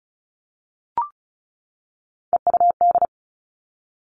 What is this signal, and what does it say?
Morse code, heard as a keyed tone, about 36 wpm, 700 Hz. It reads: EUD
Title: Three Challenging Letter-Number Combinations 35wpm